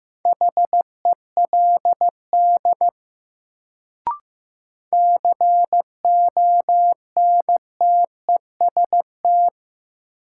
Morse code, heard as a keyed tone, about 15 wpm, 700 Hz. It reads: HELD CONTEST
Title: Sentences from Top 1600 Words - New Word Drill - 15wpm